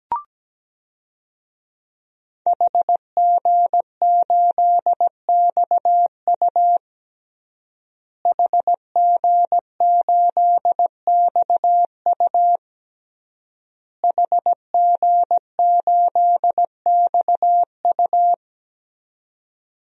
Morse code, heard as a keyed tone, about 17 wpm, 700 Hz. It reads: HG8XU HG8XU HG8XU
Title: DXCC-100 Call Signs - 5 Characters - Repeated 3x - 17wpm